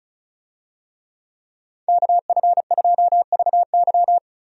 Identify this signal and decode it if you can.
Morse code, heard as a keyed tone, about 34 wpm, 700 Hz. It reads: KF2VY
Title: USA Call Signs - 5 Characters - 35wpm